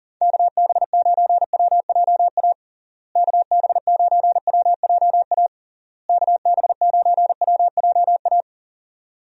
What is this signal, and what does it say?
Morse code, heard as a keyed tone, about 40 wpm, 700 Hz. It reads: KB9WJA KB9WJA KB9WJA
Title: USA Call Signs - 6 Characters - Repeated 3x - 40wpm